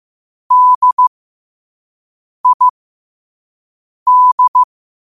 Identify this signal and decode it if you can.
Morse code, heard as a keyed tone, about 15 wpm, spaced as for 5 wpm, 1000 Hz. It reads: DID
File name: qotdfiles / 5wpm